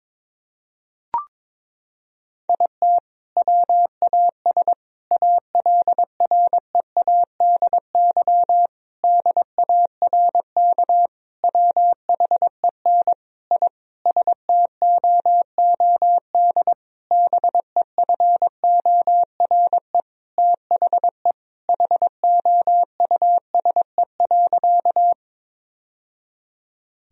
Morse code, heard as a keyed tone, about 22 wpm, 700 Hz. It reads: IT WAS ALREADY DARK WHEN I STOOD BEFORE THE HOUSE.
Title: Sentences from Top 1000 Words - Review Part 2 - 22wpm